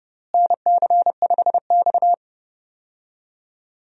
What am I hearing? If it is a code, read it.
Morse code, heard as a keyed tone, about 31 wpm, 700 Hz. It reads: NC5X
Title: USA Call Signs - 4 Characters - Repeated 3x - 30wpm